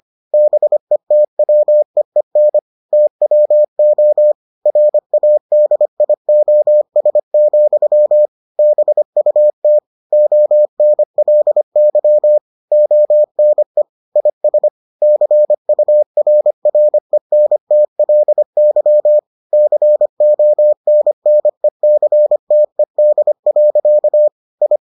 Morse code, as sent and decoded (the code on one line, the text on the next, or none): -... - .-- . -. / - .-- --- / .-. .- -.. .. --- ... --..-- / -... ..- - / --- -. .-.. -.-- / --- -. / .. ... / -.-. ..- .-. .-. -. - .-.. -.-- / -.-. --- -. -. -.-. - . -.. .-.-.- / ..
BETWEEN TWO RADIOS, BUT ONLY ONE IS CURRENTLY CONNECTED. I